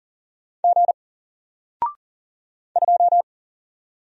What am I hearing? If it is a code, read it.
Morse code, heard as a keyed tone, about 40 wpm, 700 Hz. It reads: G 2
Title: Single Letter-Number Alt 40wpm